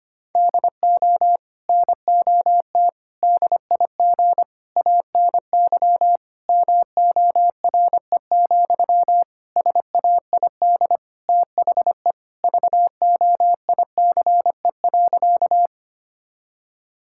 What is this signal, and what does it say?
Morse code, heard as a keyed tone, about 25 wpm, 700 Hz. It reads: DO NOT DIG ANY MORE, SAID THE VOICE.
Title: Sentences from Top 1600 Words 25wpm